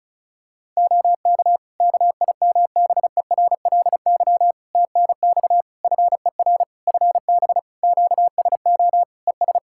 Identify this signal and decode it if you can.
Morse code, heard as a keyed tone, about 36 wpm, 700 Hz. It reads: OK KIMBERLY TNX FER FB QSO ES
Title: Practice QSO - Exchange 4 - Ending - 35wpm